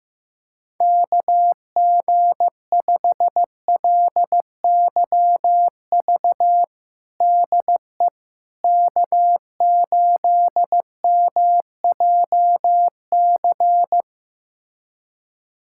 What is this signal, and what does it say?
Morse code, heard as a keyed tone, about 16 wpm, 700 Hz. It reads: KG5LYV DE K8MJC
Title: USA Callsign DE Callsign 15wpm